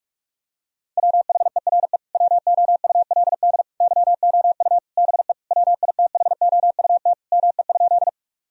Morse code, heard as a keyed tone, about 46 wpm, 700 Hz. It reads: WHERE WOULD YOU BE WITHOUT ME?
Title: Sentences from Top 1000 Words - Review Part 3 - 45wpm